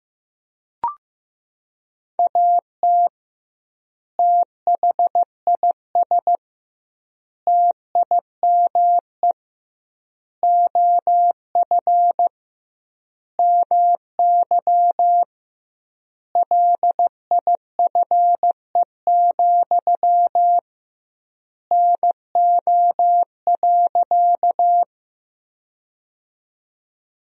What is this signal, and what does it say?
Morse code, heard as a keyed tone, about 16 wpm, 700 Hz. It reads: AT THIS TIME OF MY LIFE, NO.
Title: 2x Word-Spacing - Sentences from Top 300 Words - 15wpm